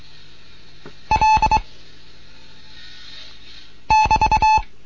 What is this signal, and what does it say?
Morse code, heard as a keyed tone, about 22 wpm, 870 Hz. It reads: L <BT>